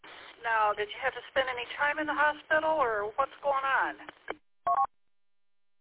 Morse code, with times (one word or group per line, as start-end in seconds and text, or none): none